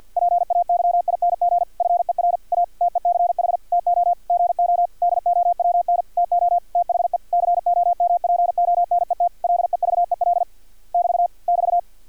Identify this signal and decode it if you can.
Morse code, heard as a keyed tone, about 50 wpm, 700 Hz. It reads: PAYING FER A TECH TO GO DOWN TO THE COMPONET LEVEL <BT> <BT>
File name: archived / CRSnet